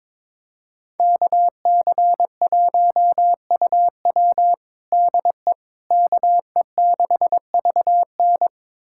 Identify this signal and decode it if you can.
Morse code, heard as a keyed tone, about 22 wpm, 700 Hz. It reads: KC1UW DE KE6VN
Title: USA Callsign DE Callsign - 5 Characters - 22wpm